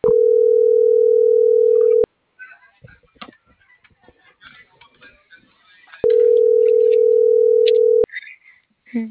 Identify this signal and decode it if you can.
Ringback tone (North America), 440 + 480 Hz.